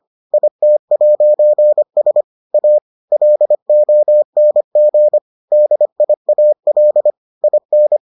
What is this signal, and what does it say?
Morse code, heard as a keyed tone, about 25 wpm, 600 Hz. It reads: IT'S A LONG DIAL IN